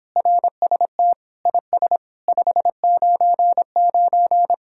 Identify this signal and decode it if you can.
Morse code, heard as a keyed tone, about 31 wpm, 700 Hz. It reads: RST IS 599